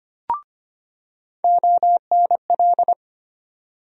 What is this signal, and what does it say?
Morse code, heard as a keyed tone, about 25 wpm, 700 Hz. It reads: ONL
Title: USA Call Sign Postfixes - 3 Letters 25wpm